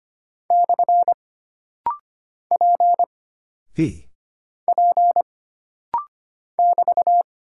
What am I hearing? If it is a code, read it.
Morse code, heard as a keyed tone, about 25 wpm, 700 Hz. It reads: / P P <BT>
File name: Single Character (Letter:Number:Punctuation:Prosign) - Rapid-Fire - 25wpm